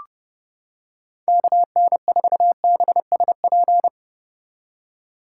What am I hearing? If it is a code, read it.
Morse code, heard as a keyed tone, about 30 wpm, 700 Hz. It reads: KN4BSP